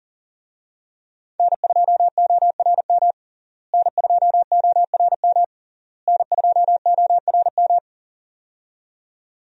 Morse code, heard as a keyed tone, about 40 wpm, 700 Hz. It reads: N2ORM N2ORM N2ORM
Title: USA Call Signs - 5 Characters - Repeated 3x - 40wpm